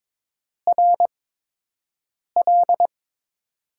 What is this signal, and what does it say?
Morse code, heard as a keyed tone, about 22 wpm, spaced as for 6 wpm, 700 Hz. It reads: RL
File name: Top 90 QSO Elements - 8x Character Spacing - 22wpm